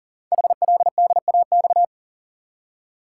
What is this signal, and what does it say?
Morse code, heard as a keyed tone, about 41 wpm, 690 Hz. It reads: HLDAX
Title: Code Groups - 5 Letters - 40wpm